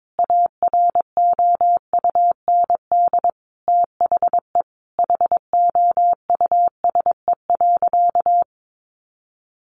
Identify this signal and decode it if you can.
Morse code, heard as a keyed tone, about 22 wpm, 700 Hz. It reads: AROUND THE HOUSE.
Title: Easy Phrases 22wpm